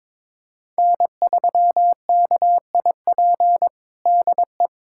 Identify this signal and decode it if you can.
Morse code, heard as a keyed tone, about 22 wpm, 700 Hz. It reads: N3KIP DE
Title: USA Callsign DE Callsign - 5 Characters - 22wpm